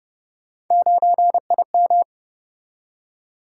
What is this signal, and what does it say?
Morse code, heard as a keyed tone, about 28 wpm, 700 Hz. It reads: 9IM